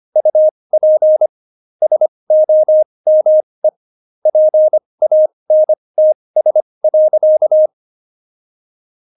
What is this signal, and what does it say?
Morse code, heard as a keyed tone, about 25 wpm, spaced as for 20 wpm, 620 Hz. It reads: UP SOME PANTS.
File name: Jokes-25-char-20-eff